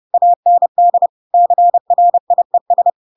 Morse code, heard as a keyed tone, about 30 wpm, 700 Hz. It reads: AND CRIES
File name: Pinocchio / Pinocchio ch7 30 30 ik1hge